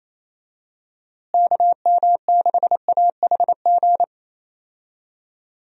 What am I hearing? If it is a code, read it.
Morse code, heard as a keyed tone, about 28 wpm, 700 Hz. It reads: KM6AHG
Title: USA Call Signs - 6 Characters - Repeated 2x - 28wpm